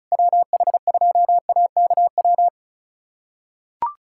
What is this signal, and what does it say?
Morse code, heard as a keyed tone, about 34 wpm, 700 Hz. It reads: WH2AKW
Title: USA Callsign DE Callsign - 6 Characters - 35wpm